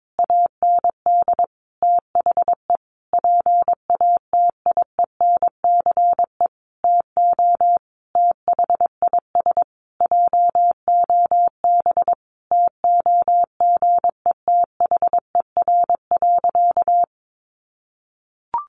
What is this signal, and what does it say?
Morse code, heard as a keyed tone, about 22 wpm, 700 Hz. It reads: AND THE PATIENCE TO THIS JOB TOGETHER.
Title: Sentences from Top 1300 Words 22wpm